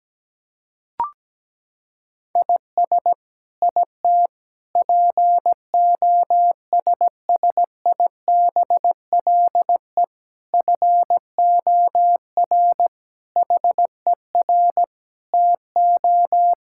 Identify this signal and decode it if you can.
Morse code, heard as a keyed tone, about 17 wpm, 700 Hz. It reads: IS IT POSSIBLE FOR HER TO
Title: Sentences from Top 1000 Words - Review Part 6 - 17wpm